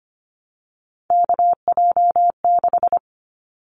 Morse code, heard as a keyed tone, about 25 wpm, 700 Hz. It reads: KJ6
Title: Callsign Prefixes for Top 25 DXCC - Repeated 3x - 25wpm